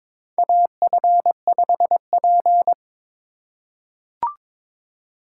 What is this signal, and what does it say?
Morse code, heard as a keyed tone, about 22 wpm, 700 Hz. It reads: AF5P
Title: K1USN SST QSO Practice - 22wpm